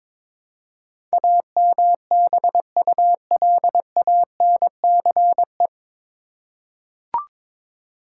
Morse code, heard as a keyed tone, about 22 wpm, 700 Hz. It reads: AMBULANCE
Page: Morse Code Ninja practice 22wpm